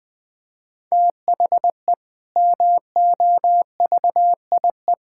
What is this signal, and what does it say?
Morse code, heard as a keyed tone, about 20 wpm, 700 Hz. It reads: THE MOVIE
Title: Sentences with 6-Word Phrases - 20wpm